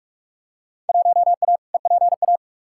Morse code, heard as a keyed tone, about 46 wpm, 700 Hz. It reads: 1A EPA
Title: Field Day Exchanges - 45wpm